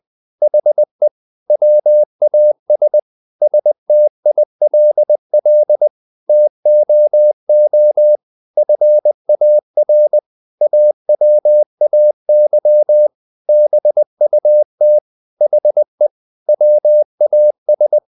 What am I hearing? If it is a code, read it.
Morse code, heard as a keyed tone, about 20 wpm, 600 Hz. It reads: HE WAS STILL TOO FAR AWAY BUT HE WAS
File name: the old man and the sea part 2 20wpm